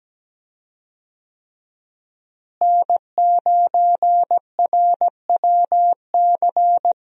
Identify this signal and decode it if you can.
Morse code, heard as a keyed tone, about 17 wpm, 700 Hz. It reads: N9RWC
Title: USA Call Signs - 5 Characters - Repeated 3x - 17wpm